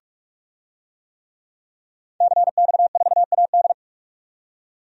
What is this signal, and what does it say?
Morse code, heard as a keyed tone, about 46 wpm, 700 Hz. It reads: CX4AD
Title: DXCC-100 Call Signs - 5 Characters - 45wpm